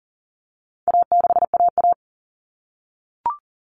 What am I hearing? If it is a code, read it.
Morse code, heard as a keyed tone, about 40 wpm, 700 Hz. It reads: A6AA
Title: All Cut Numbers - 4-Digit 40wpm